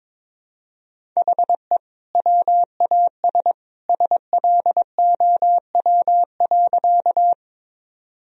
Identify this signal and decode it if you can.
Morse code, heard as a keyed tone, about 22 wpm, 700 Hz. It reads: HE WAS SLOW.
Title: Sentences from Top 1000 Words - Review Part 2 - 22wpm